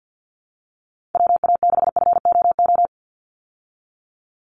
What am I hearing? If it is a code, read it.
Morse code, heard as a keyed tone, about 50 wpm, 700 Hz. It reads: WA6FOO
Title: USA Call Signs 50wpm